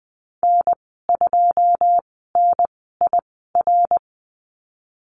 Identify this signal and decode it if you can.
Morse code, heard as a keyed tone, about 20 wpm, spaced as for 15 wpm, 700 Hz. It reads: N2NIR
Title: 167 Lesson 27g - Number 7 - Call Signs 20wpm